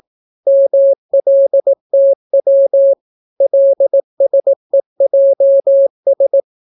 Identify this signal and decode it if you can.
Morse code, heard as a keyed tone, about 18 wpm, 550 Hz. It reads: MLTW LSEJS